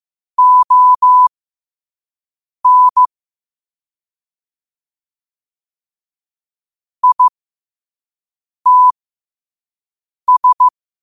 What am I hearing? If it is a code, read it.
Morse code, heard as a keyed tone, about 15 wpm, spaced as for 5 wpm, 1000 Hz. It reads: ON ITS
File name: 5wpm